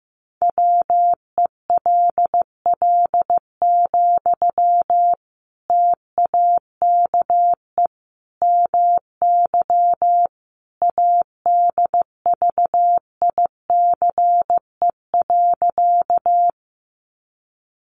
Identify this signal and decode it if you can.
Morse code, heard as a keyed tone, about 15 wpm, 700 Hz. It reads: WELL, TAKE MY ADVICE.
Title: Sentences from Top 1100 Words 15wpm